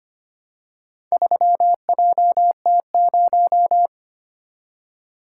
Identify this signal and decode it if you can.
Morse code, heard as a keyed tone, about 25 wpm, 700 Hz. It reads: 3JT0
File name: practice / 25wpm / Four Letter-Number Combinations - 25wpm